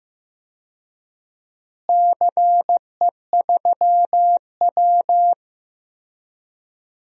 Morse code, heard as a keyed tone, about 15 wpm, 700 Hz. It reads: CE3W
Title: DXCC-100 Call Signs - 4 Characters - Repeated 3x - 15wpm